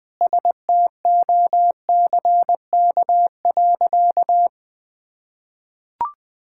Morse code, heard as a keyed tone, about 20 wpm, 700 Hz. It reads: STOCK.
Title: Sentences from Top 1000 Words - Review Part 5 - 20wpm